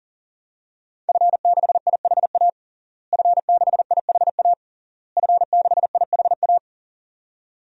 Morse code, heard as a keyed tone, about 40 wpm, 700 Hz. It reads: F6IHA F6IHA F6IHA
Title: DXCC-100 Call Signs - 5 Characters - Repeated 3x - 40wpm